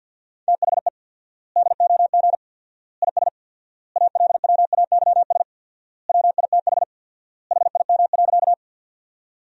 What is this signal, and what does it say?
Morse code, heard as a keyed tone, about 49 wpm, 700 Hz. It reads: THE DOG IS ALWAYS WITH HIM.
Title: 4x Word-Spacing - Sentences from Top 400 Words - 50wpm